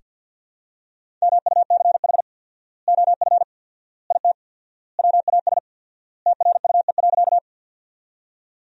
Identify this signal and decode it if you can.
Morse code, heard as a keyed tone, about 45 wpm, 700 Hz. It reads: MUCH OF IT WAS TRUE.
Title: 4x Word-Spacing - Sentences from Top 400 Words - 50wpm